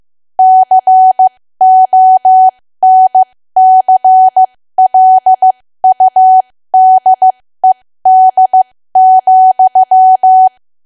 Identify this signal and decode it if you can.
Morse code, heard as a keyed tone, about 15 wpm, spaced as for 13 wpm, 750 Hz. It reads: CONCLUDED,
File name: cwa / W1AW-Level2 / 13WPM